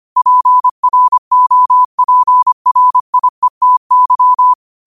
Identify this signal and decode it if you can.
Morse code, heard as a keyed tone, about 25 wpm, 1000 Hz. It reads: PROPRIETY
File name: qotdfiles / 25wpm